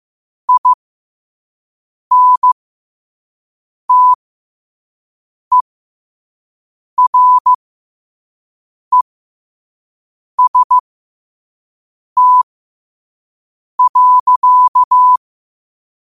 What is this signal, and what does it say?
Morse code, heard as a keyed tone, about 15 wpm, spaced as for 5 wpm, 1000 Hz. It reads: INTEREST.
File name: qotdfiles / 5wpm